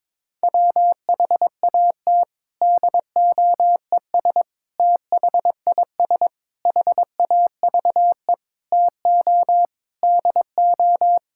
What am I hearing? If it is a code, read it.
Morse code, heard as a keyed tone, about 22 wpm, 700 Hz. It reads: WHAT DOES THIS HAVE TO DO
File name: practice / 22wpm / Sets of 7 Words - 22wpm